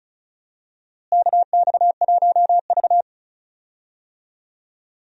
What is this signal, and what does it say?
Morse code, heard as a keyed tone, about 36 wpm, 700 Hz. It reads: KX1V
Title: USA Call Signs - 4 Characters - Repeated 3x - 35wpm